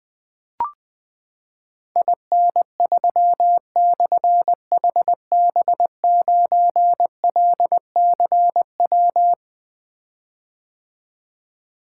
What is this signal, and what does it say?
Morse code, heard as a keyed tone, about 21 wpm, 700 Hz. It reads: IN3/HB9LCW
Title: Difficult Call Signs 20wpm